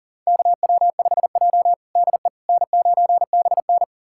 Morse code, heard as a keyed tone, about 37 wpm, 700 Hz. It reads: KW5J DE N9BN
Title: USA Callsign DE Callsign - 4 Characters - 40wpm